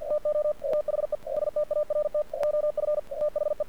Morse code, heard as a keyed tone, about 51 wpm, 600 Hz, with radio noise. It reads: TO THE START OF THE
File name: archived / CRSnet